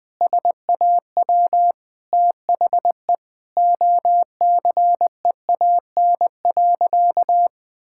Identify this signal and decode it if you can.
Morse code, heard as a keyed tone, about 19 wpm, 700 Hz. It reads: SAW THE OCEAN.
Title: Sentences with 8-Word Phrases - 20wpm